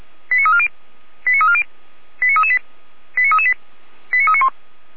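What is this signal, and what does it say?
Five-tone selective call (ZVEI), heard as quick runs of five tones, tones of about 70 ms. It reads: 89340 89340 89338 89338 89391
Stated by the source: ZVEI five-tone selective call received on an SDR.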